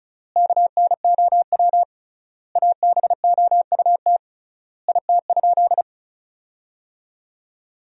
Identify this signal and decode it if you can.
Morse code, heard as a keyed tone, about 34 wpm, 700 Hz. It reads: KNOW ABOUT IT?
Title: 3x Word-Spacing - Sentences from Top 100 Words - 35wpm